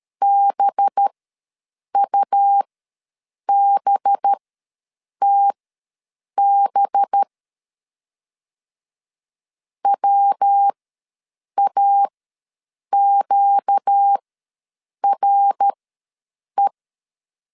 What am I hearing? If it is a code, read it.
Morse code, heard as a keyed tone, about 13 wpm, spaced as for 7 wpm, 800 Hz. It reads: BUBTB WAQRE
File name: 5 LB ws 12wpm